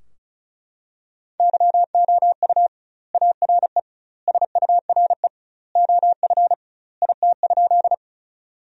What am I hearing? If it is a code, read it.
Morse code, heard as a keyed tone, about 36 wpm, 700 Hz. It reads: YOU ARE SURE OF IT?
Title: 2x Word-Spacing - Sentences from Top 300 Words - 35wpm